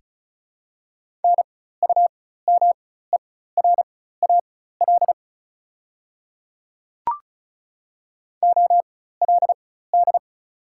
Morse code, heard as a keyed tone, about 36 wpm, spaced as for 16 wpm, 690 Hz. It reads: NUMERAL OLD
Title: 4x Character Spacing - Top 500 Words - 35wpm